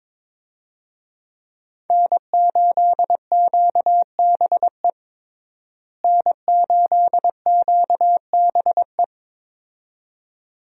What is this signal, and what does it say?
Morse code, heard as a keyed tone, about 22 wpm, 700 Hz. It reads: N8QBE N8QBE
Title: USA Call Signs - 5 Characters - Repeated 2x - 22wpm